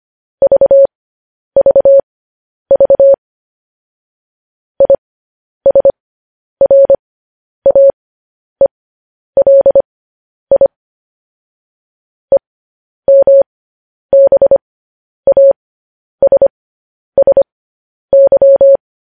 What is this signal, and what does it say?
Morse code, heard as a keyed tone, about 25 wpm, spaced as for 10 wpm, 550 Hz. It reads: VVV ISRAELI EMBASSY
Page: news-headlines-in-morse-code-at-10-wpm